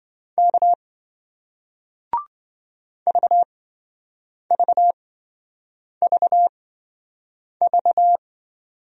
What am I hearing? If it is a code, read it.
Morse code, heard as a keyed tone, about 27 wpm, spaced as for 7 wpm, 700 Hz. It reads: K VVVV